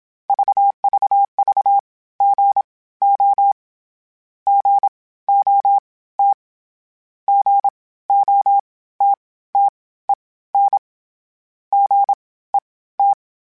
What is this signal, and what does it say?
Morse code, heard as a keyed tone, about 26 wpm, spaced as for 16 wpm, 800 Hz. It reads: VVV GO GOT GOTTEN GET
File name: Short Runs / Short Runs-mp3s / Gotten @25